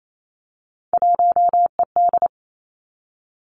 Morse code, heard as a keyed tone, about 28 wpm, 700 Hz. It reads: 1ED
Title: Three Challenging Letter-Number Combinations 28wpm